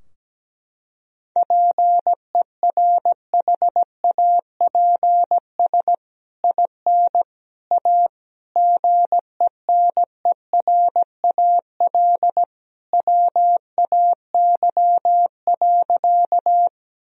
Morse code, heard as a keyed tone, about 17 wpm, 700 Hz. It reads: PERHAPS IN A GENERAL WAY.
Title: Sentences from Top 1000 Words - Review Part 3 - 17wpm